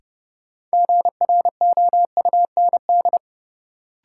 Morse code, heard as a keyed tone, about 31 wpm, 700 Hz. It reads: GROUND
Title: Sets of 3 Words - New Word Drill - 30wpm